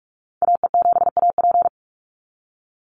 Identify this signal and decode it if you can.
Morse code, heard as a keyed tone, about 44 wpm, 700 Hz. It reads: AE7AP
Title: NA SOTA Chaser Callsigns 45wpm